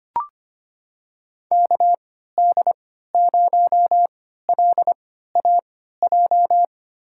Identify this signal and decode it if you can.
Morse code, heard as a keyed tone, about 25 wpm, spaced as for 14 wpm, 700 Hz. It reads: KD0LAJ
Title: USA Call Signs - 3x Character Spacing - 25wpm